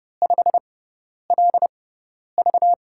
Morse code, heard as a keyed tone, about 29 wpm, spaced as for 10 wpm, 700 Hz. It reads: 5LV
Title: USA Call Signs - 5 Characters - 6x Character Spacing - 30wpm